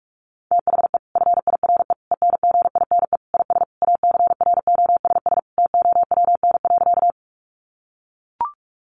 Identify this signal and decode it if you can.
Morse code, heard as a keyed tone, about 44 wpm, 700 Hz. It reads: THE FIRE ENGINE IS ACROSS TOWN.
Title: Sentences from Top 1500 Words 45wpm